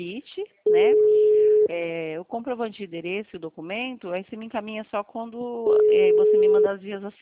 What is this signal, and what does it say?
Ringback tone (Europe, 425 Hz), with speech.